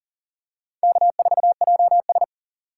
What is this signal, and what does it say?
Morse code, heard as a keyed tone, about 40 wpm, 700 Hz. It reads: K4JS
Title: USA Call Signs - 4 Characters - 2x Character Spacing - 40wpm